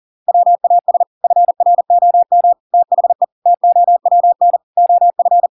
Morse code, heard as a keyed tone, about 40 wpm, 700 Hz. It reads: WAS FROM THE TOWN OF